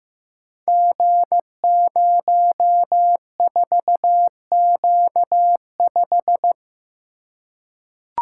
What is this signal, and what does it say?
Morse code, heard as a keyed tone, about 15 wpm, 700 Hz. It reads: G04Q5
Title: Code Groups - 5 Letter:Numbers - 15wpm